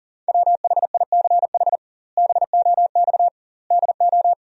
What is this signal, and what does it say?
Morse code, heard as a keyed tone, about 38 wpm, 700 Hz. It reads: WHICH BOX DO
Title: 2x Word-Spacing - Sentences from Top 500 Words - 40wpm